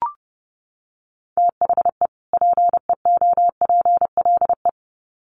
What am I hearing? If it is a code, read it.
Morse code, heard as a keyed tone, about 30 wpm, 700 Hz. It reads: THE PEOPLE
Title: Sets of 2 Words 30wpm